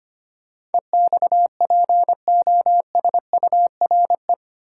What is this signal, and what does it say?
Morse code, heard as a keyed tone, about 25 wpm, 690 Hz. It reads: EXPOSURE
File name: 8 Letter Words - First 3 Letters Given - 25wpm